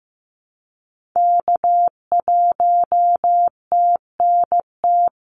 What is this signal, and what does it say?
Morse code, heard as a keyed tone, about 15 wpm, 700 Hz. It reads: K1TNT
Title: NA SOTA Activator Callsigns 15wpm